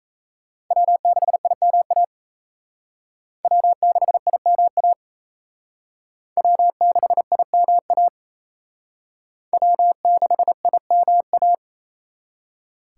Morse code, heard as a keyed tone, about 34 wpm, 700 Hz. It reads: W6IMA W6IMA W6IMA W6IMA
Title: Speed Racing - USA 1x3 Call Signs 28wpm